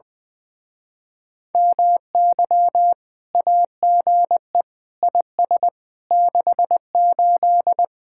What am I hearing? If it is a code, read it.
Morse code, heard as a keyed tone, about 20 wpm, 700 Hz. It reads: MY AGE IS 68
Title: Speed-Racing Common QSO Phrases - 20wpm